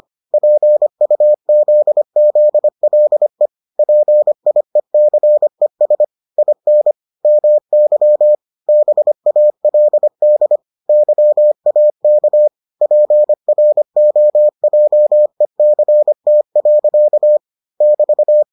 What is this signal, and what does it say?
Morse code, heard as a keyed tone, about 25 wpm, 600 Hz. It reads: PUZZLE PIECES IN MY BALD YAK PROJECT. <BT>